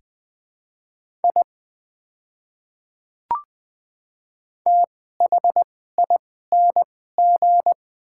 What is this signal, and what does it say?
Morse code, heard as a keyed tone, about 21 wpm, spaced as for 15 wpm, 700 Hz. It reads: I THING
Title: 2x Character Spacing - Top 100 Words - 20wpm